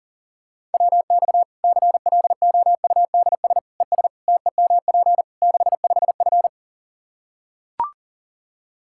Morse code, heard as a keyed tone, about 40 wpm, 700 Hz. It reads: WX CLOUDS ES TEMP 65F
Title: Practice QSO - Exchange 2 - Info - 40wpm